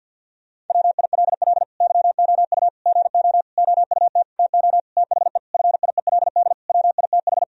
Morse code, heard as a keyed tone, about 50 wpm, 700 Hz. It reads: WILL YOU GO OUT TO THE FIELD WITH
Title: Sentences from Top 1000 Words - Review Part 2 - 50wpm